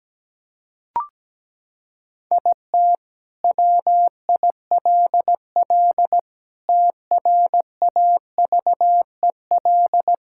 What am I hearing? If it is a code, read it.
Morse code, heard as a keyed tone, about 17 wpm, 700 Hz. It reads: IT WILL TRAVEL
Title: Sentences from Top 1000 Words - Review Part 6 - 17wpm